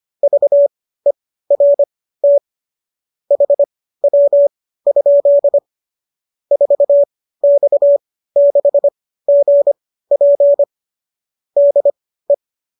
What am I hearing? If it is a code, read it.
Morse code, heard as a keyed tone, about 25 wpm, spaced as for 15 wpm, 570 Hz. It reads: VERT HW? 4X6GP DE